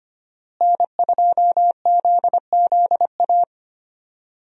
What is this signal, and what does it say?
Morse code, heard as a keyed tone, about 25 wpm, 700 Hz. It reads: N2ZZA
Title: USA Call Signs - 5 Characters - Repeated 2x - 25wpm